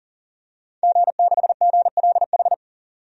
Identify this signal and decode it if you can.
Morse code, heard as a keyed tone, about 38 wpm, 700 Hz. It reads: G6GLH
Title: DXCC-100 Call Signs - 5 Characters - 40wpm